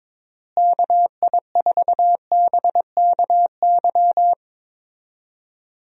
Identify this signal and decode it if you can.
Morse code, heard as a keyed tone, about 22 wpm, 700 Hz. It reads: KI4BKY
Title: USA Call Signs - 6 Characters - Repeated 2x - 22wpm